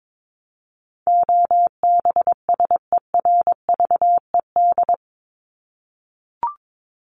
Morse code, heard as a keyed tone, about 22 wpm, 700 Hz. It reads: OBSERVED
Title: ED Suffix - Top 100 Words with Base Words 22wpm